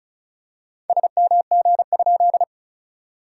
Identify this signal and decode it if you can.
Morse code, heard as a keyed tone, about 36 wpm, 660 Hz. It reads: SMG?
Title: SOTA QSO Practice 35wpm